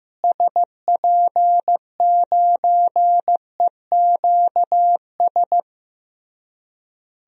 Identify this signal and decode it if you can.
Morse code, heard as a keyed tone, about 15 wpm, 700 Hz. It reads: SP9EQS
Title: DXCC-100 Call Signs - 6 Characters - Repeated 3x - 15wpm